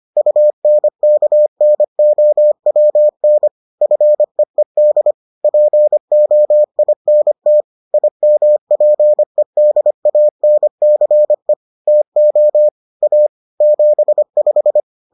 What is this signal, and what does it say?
Morse code, heard as a keyed tone, about 25 wpm, 600 Hz. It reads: UNKNOWN FEED POINT IMPEDANCE TO A 75